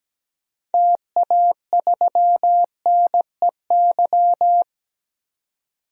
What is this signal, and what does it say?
Morse code, heard as a keyed tone, about 17 wpm, 700 Hz. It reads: TA3NEY
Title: DXCC-100 Call Signs - 6 Characters - 17wpm